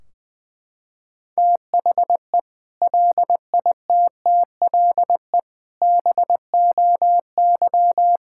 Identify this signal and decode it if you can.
Morse code, heard as a keyed tone, about 20 wpm, 700 Hz. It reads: THE LITTLE BOY
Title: Sentences from Top 1600 Words 20wpm